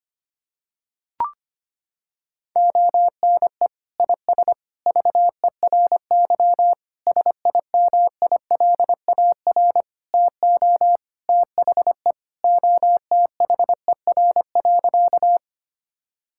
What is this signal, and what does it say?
Morse code, heard as a keyed tone, about 25 wpm, 700 Hz. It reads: ONE IS VERY SIMILAR TO THE OTHER.
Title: Sentences from Top 1000 Words - Review Part 5 - 25wpm